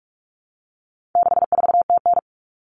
Morse code, heard as a keyed tone, about 44 wpm, 700 Hz. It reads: 64TN
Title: Common Cut Numbers - 4-Digit 45wpm